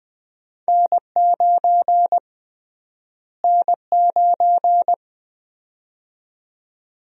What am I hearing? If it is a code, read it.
Morse code, heard as a keyed tone, about 20 wpm, 700 Hz. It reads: N9 N9
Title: USA Call Sign Prefixes - Repeated 2x - 20wpm